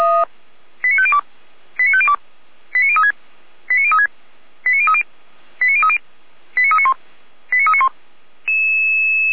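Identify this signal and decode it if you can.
Five-tone selective call (ZVEI), heard as quick runs of five tones, tones of about 70 ms. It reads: 89502 89502 89036 89036 89030 89030 89391 89391
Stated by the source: Five-tone ZVEI selective call 89207 received by SDR, channel 455.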